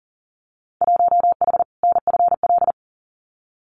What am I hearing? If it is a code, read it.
Morse code, heard as a keyed tone, about 40 wpm, 700 Hz. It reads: 1H NFL
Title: Winter Field Day Exchanges 40wpm